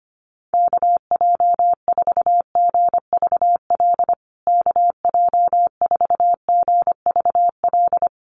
Morse code, heard as a keyed tone, about 25 wpm, 700 Hz. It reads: KJ4GVL KJ4GVL
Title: USA Call Signs Repeated 3x 25wpm